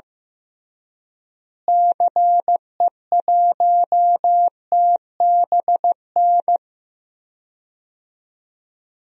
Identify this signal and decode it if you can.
Morse code, heard as a keyed tone, about 16 wpm, 700 Hz. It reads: CE1TBN
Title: DXCC-100 Call Signs - 6 Characters - Repeated 3x - 15wpm